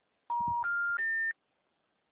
Special information tone.